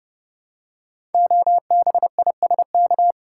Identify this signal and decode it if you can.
Morse code, heard as a keyed tone, about 28 wpm, 700 Hz. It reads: OBISK
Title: Code Groups - 5 Letters - 30wpm